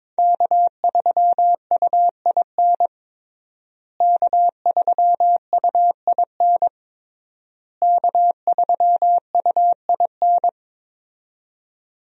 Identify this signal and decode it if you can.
Morse code, heard as a keyed tone, about 21 wpm, 700 Hz. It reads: K3UIN K3UIN K3UIN
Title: USA Call Signs - 5 Characters - Repeated 3x - 22wpm